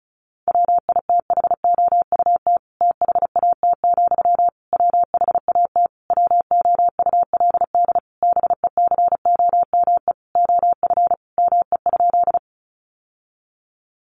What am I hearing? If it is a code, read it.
Morse code, heard as a keyed tone, about 34 wpm, 700 Hz. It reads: WITHOUT THAT, WHAT WOULD BECOME OF ME?
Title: Sentences from Top 600 Words 35wpm